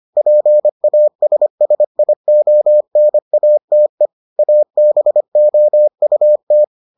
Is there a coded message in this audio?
Yes: Morse code from 0.2 to 6.6 s, heard as a keyed tone, about 25 wpm, 600 Hz. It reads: PASSIONATE ABOUT